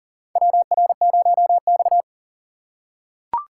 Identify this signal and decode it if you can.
Morse code, heard as a keyed tone, about 40 wpm, 700 Hz. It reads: WR0X
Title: USA Call Signs - 6x Character Spacing - 40wpm